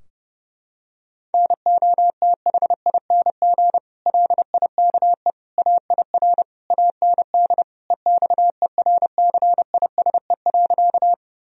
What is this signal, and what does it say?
Morse code, heard as a keyed tone, about 30 wpm, 700 Hz. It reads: NOTHING LIKE AIR AND EXERCISE.
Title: Sentences from Top 1000 Words - Review Part 4 - 30wpm